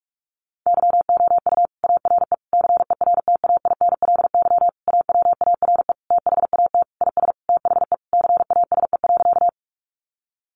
Morse code, heard as a keyed tone, about 44 wpm, 700 Hz. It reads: YOU ARE CERTAINLY AWARE THAT IS THE CASE.
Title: Sentences from Top 1200 Words 45wpm